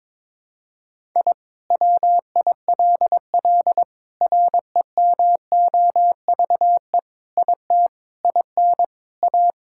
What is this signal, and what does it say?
Morse code, heard as a keyed tone, about 22 wpm, 700 Hz. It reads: I WILL REMOVE IT IN A
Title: Sentences from Top 1000 Words - Review Part 5 - 22wpm